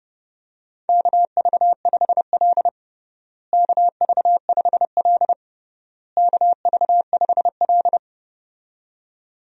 Morse code, heard as a keyed tone, about 30 wpm, 700 Hz. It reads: KV5L KV5L KV5L
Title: USA Call Signs - 4 Characters - Repeated 3x - 30wpm